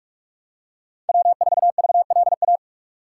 Morse code, heard as a keyed tone, about 43 wpm, 700 Hz. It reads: W4VLA
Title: USA Call Signs - 5 Characters - 2x Character Spacing - 45wpm